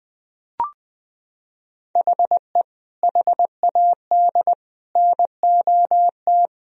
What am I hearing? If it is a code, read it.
Morse code, heard as a keyed tone, about 20 wpm, 700 Hz. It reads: HE HAD NOT
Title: Sentences from Top 1000 Words - Review Part 4 - 20wpm